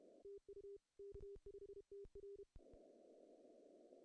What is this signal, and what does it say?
Morse code, heard as a keyed tone, about 31 wpm, 410 Hz. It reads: TU K5TR
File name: CW / cqww-cw-k5tr